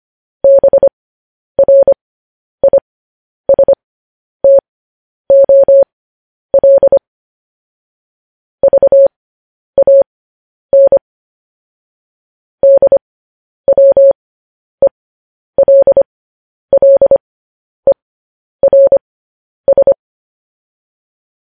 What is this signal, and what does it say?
Morse code, heard as a keyed tone, about 25 wpm, spaced as for 10 wpm, 550 Hz. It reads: BRISTOL VAN DWELLERS
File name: podcast / News headlines-1x-25x10